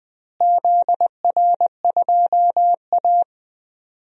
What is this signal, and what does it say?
Morse code, heard as a keyed tone, about 21 wpm, 700 Hz. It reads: ZR2A
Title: DXCC-100 Call Signs - 4 Characters - Repeated 2x - 20wpm